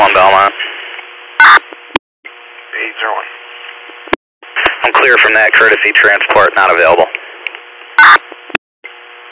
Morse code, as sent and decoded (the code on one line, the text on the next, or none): none